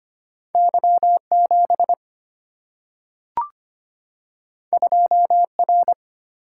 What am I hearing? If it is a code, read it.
Morse code, heard as a keyed tone, about 25 wpm, 700 Hz. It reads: Y7 2R
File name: Two Letter-Number Combinations - 25wpm